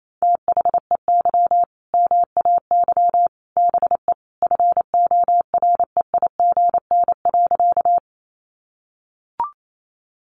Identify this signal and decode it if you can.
Morse code, heard as a keyed tone, about 28 wpm, 700 Hz. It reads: THEY MAY BE FOREIGN.
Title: Sentences from Top 1400 Words 28wpm